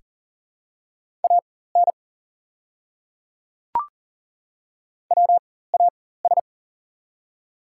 Morse code, heard as a keyed tone, about 40 wpm, spaced as for 19 wpm, 700 Hz. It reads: AN WAS
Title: 4x Character Spacing - Top 100 Words - 40wpm